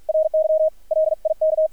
Morse code, heard as a keyed tone, about 53 wpm, 630 Hz, with radio noise. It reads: RY LIK
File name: archived / CRSnet